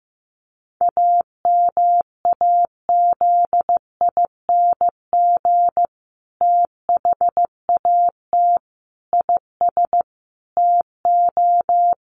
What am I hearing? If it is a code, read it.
Morse code, heard as a keyed tone, about 15 wpm, 700 Hz. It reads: AMAZING THAT IS TO
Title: Sentences from Top 1500 Words 15wpm